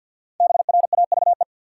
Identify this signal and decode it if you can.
Morse code, heard as a keyed tone, about 50 wpm, 700 Hz. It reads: BRAVE
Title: Five Letter Words - 50wpm